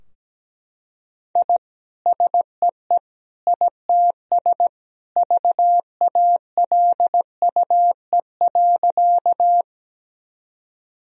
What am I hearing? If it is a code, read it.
Morse code, heard as a keyed tone, about 17 wpm, 700 Hz. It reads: I SEE ITS VALUE.
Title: Sentences from Top 1000 Words - Review Part 3 - 17wpm